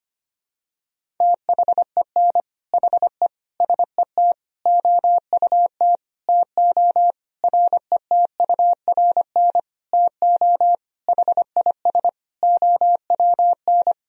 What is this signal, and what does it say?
Morse code, heard as a keyed tone, about 25 wpm, 700 Hz. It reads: THEN HE SET OUT TO RETURN TO HIS OWN
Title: Sentences from Top 1000 Words - Review Part 1 - 25wpm